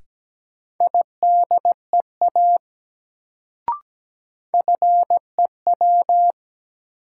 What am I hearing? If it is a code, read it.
Morse code, heard as a keyed tone, about 17 wpm, 700 Hz. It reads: IDEA FEW
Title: Sentences from Top 300 Words - New Word Drill - 17wpm